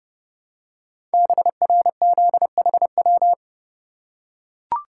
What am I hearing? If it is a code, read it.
Morse code, heard as a keyed tone, about 30 wpm, 700 Hz. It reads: BRZHW
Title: Code Groups - 5 Letters - 30wpm